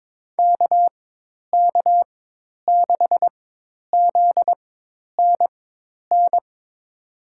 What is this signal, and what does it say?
Morse code, heard as a keyed tone, about 22 wpm, spaced as for 10 wpm, 700 Hz. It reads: KK6ZNN
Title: USA Call Signs - 6 Characters - 4x Character Spacing - 22wpm